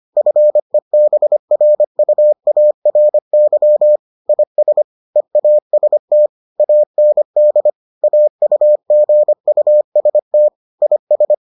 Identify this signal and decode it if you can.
Morse code, heard as a keyed tone, about 25 wpm, 600 Hz. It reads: FEBRUARY IS EAST AND AUGUST IS